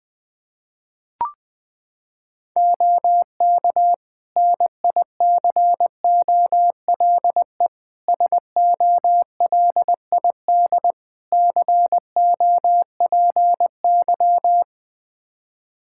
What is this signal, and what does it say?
Morse code, heard as a keyed tone, about 20 wpm, 700 Hz. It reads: OK NICOLE SOLID COPY
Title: Practice QSO - Exchange 3 - More Info - 20wpm